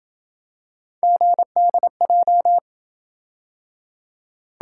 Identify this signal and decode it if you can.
Morse code, heard as a keyed tone, about 27 wpm, 700 Hz. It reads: GDJ